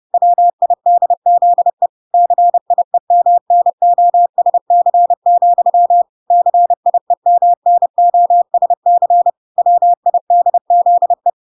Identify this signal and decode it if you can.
Morse code, heard as a keyed tone, about 30 wpm, 700 Hz. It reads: WIDZE CIEMNOSC, CIEMNOSC WIDZE